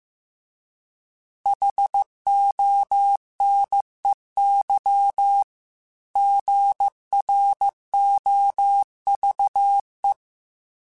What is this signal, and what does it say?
Morse code, heard as a keyed tone, about 15 wpm, 800 Hz. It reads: HONEY GROVE